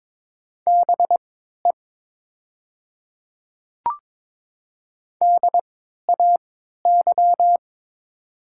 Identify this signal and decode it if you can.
Morse code, heard as a keyed tone, about 22 wpm, spaced as for 13 wpm, 700 Hz. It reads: BE DAY
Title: 3x Character Spacing - Top 100 Words - 22wpm